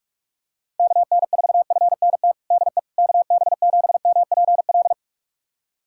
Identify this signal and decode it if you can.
Morse code, heard as a keyed tone, about 46 wpm, 700 Hz. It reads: KN4FNT DE KB7MPL
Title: USA Callsign DE Callsign - 6 Characters - 45wpm